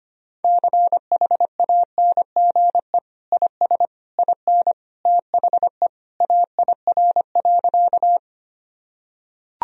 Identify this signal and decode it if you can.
Morse code, heard as a keyed tone, about 25 wpm, 700 Hz. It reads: CHANGE IS IN THE AIR.
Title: Sentences from Top 1000 Words - Review Part 1 - 25wpm